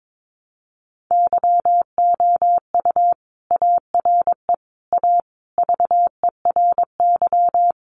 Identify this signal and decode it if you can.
Morse code, heard as a keyed tone, about 22 wpm, 700 Hz. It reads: YOU ARE A VERY
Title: Sentences from Top 1000 Words 22wpm